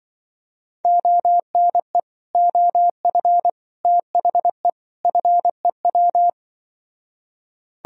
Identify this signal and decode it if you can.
Morse code, heard as a keyed tone, about 24 wpm, 700 Hz. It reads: ONE OF THE FEW
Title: Speed-Racing - Sets of 4 Words 20wpm